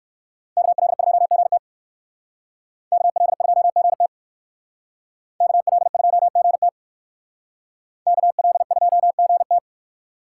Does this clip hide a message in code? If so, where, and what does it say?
Yes: Morse code from 0.6 to 9.6 s, heard as a keyed tone, about 58 wpm, 700 Hz. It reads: KL2GT KL2GT KL2GT KL2GT